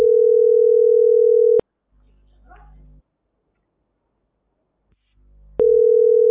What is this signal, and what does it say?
Ringback tone (North America), 440 + 480 Hz.